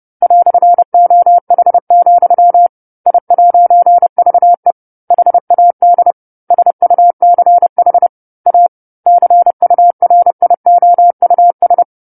Morse code, heard as a keyed tone, about 30 wpm, 700 Hz. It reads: "OH, I'VE HAD SUCH A CURIOUS
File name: 10-Chapter-10 30 WPM